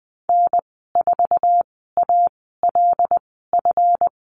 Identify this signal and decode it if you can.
Morse code, heard as a keyed tone, about 20 wpm, spaced as for 15 wpm, 700 Hz. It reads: N4ALF
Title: 173 Lesson 28g - Number 9 - Call Signs 20wpm